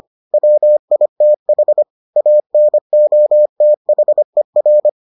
Morse code, heard as a keyed tone, about 25 wpm, 600 Hz. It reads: WITH ANOTHER